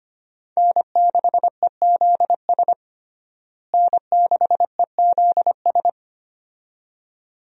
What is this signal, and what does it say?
Morse code, heard as a keyed tone, about 25 wpm, 700 Hz. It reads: N6EZS N6EZS
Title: USA Call Signs - 5 Characters - Repeated 2x - 25wpm